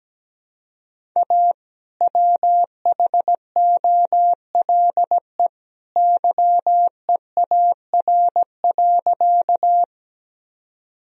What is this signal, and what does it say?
Morse code, heard as a keyed tone, about 17 wpm, 700 Hz. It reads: A WHOLE YEAR.
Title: Sentences from Top 1000 Words - Review Part 1 - 17wpm